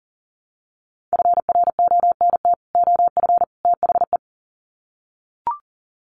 Morse code, heard as a keyed tone, about 40 wpm, 700 Hz. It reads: FRONT OF THE
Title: Sets of 3 Words 40wpm